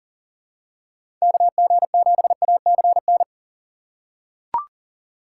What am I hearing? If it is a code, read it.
Morse code, heard as a keyed tone, about 40 wpm, 700 Hz. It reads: KG7ACN
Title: USA Call Signs - 6 Characters - 4x Character Spacing - 40wpm